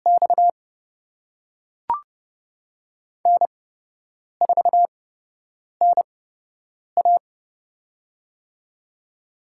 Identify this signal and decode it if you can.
Morse code, heard as a keyed tone, about 30 wpm, spaced as for 8 wpm, 700 Hz. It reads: X N4NA
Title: USA Call Signs - 8x Character Spacing - 30wpm